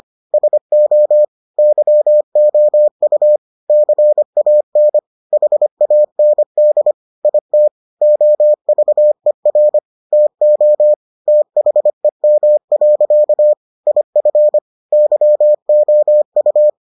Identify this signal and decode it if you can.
Morse code, heard as a keyed tone, about 25 wpm, 600 Hz. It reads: SO YOU CAN HAND IT OVER TO THEM. IF YOU